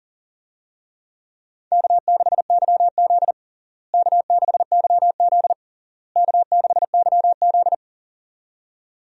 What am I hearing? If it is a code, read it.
Morse code, heard as a keyed tone, about 40 wpm, 700 Hz. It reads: K6YZ K6YZ K6YZ
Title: USA Call Signs - 4 Characters - Repeated 3x - 40wpm